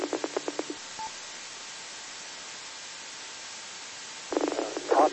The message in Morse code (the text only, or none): none